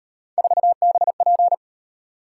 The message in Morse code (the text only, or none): 4BP